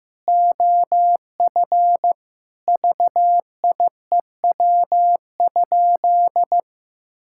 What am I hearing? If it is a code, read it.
Morse code, heard as a keyed tone, about 15 wpm, 700 Hz. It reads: OF VIEW?
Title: Sentences from Top 1000 Words - Review Part 6 - 15wpm